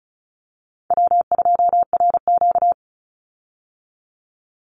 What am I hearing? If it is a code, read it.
Morse code, heard as a keyed tone, about 34 wpm, 700 Hz. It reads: W2RQ
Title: Easy Call Signs Repeated 2x - 35wpm